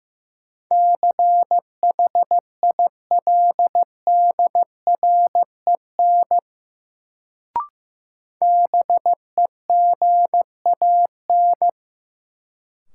Morse code, heard as a keyed tone, about 15 wpm, 700 Hz. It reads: CHILDREN BEGAN
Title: Sentences from Top 300 Words - New Word Drill - 15wpm